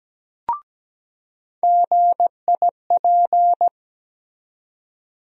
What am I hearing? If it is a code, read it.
Morse code, heard as a keyed tone, about 17 wpm, 700 Hz. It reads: GIP
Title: USA Call Sign Postfixes - 3 Letters 17wpm